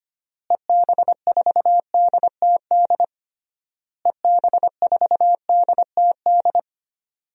Morse code, heard as a keyed tone, about 25 wpm, 700 Hz. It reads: EB4DTD EB4DTD
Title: DXCC-100 Call Signs - 6 Characters - Repeated 3x - 25wpm